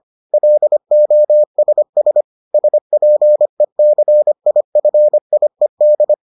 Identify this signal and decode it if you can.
Morse code, heard as a keyed tone, about 25 wpm, 600 Hz. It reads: LOSS SPECIFIED